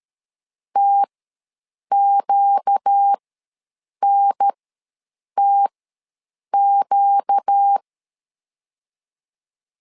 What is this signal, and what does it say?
Morse code, heard as a keyed tone, about 13 wpm, spaced as for 7 wpm, 800 Hz. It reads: TQNTQ
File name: LEZIONI / 4 Q ws 12wpm 1912